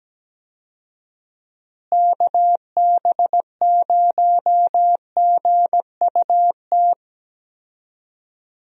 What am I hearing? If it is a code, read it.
Morse code, heard as a keyed tone, about 17 wpm, 700 Hz. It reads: KB0GUT
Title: USA Call Signs - 6 Characters - Repeated 3x - 17wpm